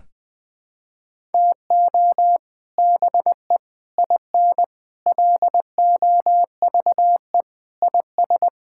Morse code, heard as a keyed tone, about 20 wpm, 700 Hz. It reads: TO BE IN LOVE IS
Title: Sentences with 3-Word Phrases - Spoken Beginning - 20wpm